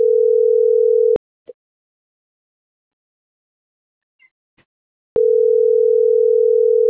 Ringback tone (North America), 440 + 480 Hz.